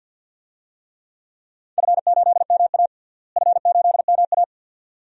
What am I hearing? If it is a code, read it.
Morse code, heard as a keyed tone, about 50 wpm, 700 Hz. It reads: F8MA F8MA
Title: DXCC-100 Call Signs - 4 Characters - Repeated 2x - 50wpm